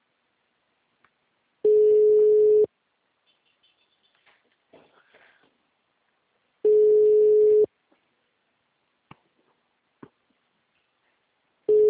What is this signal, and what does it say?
Ringback tone (Europe, 425 Hz).
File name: out- -2008- - - .811